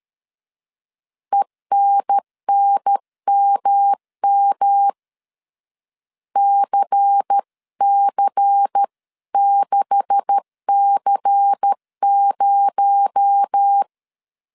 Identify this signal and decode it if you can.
Morse code, heard as a keyed tone, about 13 wpm, 800 Hz. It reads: ENNMM CC6C0
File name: CW / DET / 1 9 1500C S B 12wpm 3819